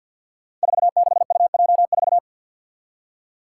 Morse code, heard as a keyed tone, about 53 wpm, 700 Hz. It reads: 46UJ4
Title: Five Challenging Letter-Number Combinations 50wpm